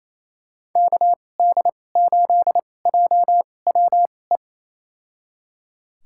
Morse code, heard as a keyed tone, about 27 wpm, spaced as for 20 wpm, 700 Hz. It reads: KD8JWE
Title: USA Call Signs - 2x Character Spacing - 28wpm